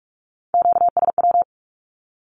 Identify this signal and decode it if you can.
Morse code, heard as a keyed tone, about 44 wpm, 700 Hz. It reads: QSW
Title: Q-Codes 45wpm